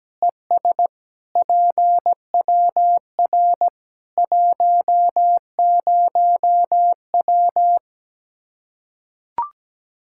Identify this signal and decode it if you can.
Morse code, heard as a keyed tone, about 17 wpm, 700 Hz. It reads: ES PWR 10W
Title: Practice QSO - Exchange 2 - Info - 17wpm